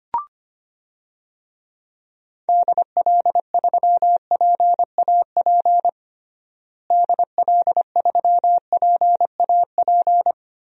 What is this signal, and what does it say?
Morse code, heard as a keyed tone, about 25 wpm, 700 Hz. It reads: DL3PAP DL3PAP
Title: DXCC-100 Call Signs - 6 Characters - Repeated 2x - 25wpm